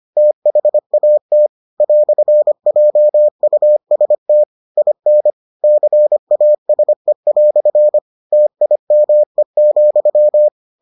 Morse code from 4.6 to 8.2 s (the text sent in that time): IN CASE"